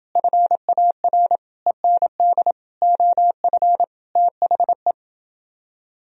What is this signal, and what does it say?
Morse code, heard as a keyed tone, about 28 wpm, 700 Hz. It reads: FAR END OF THE